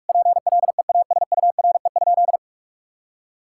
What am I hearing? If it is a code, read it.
Morse code, heard as a keyed tone, about 46 wpm, 700 Hz. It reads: PLEASURE?
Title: Sentences from Top 1600 Words 45wpm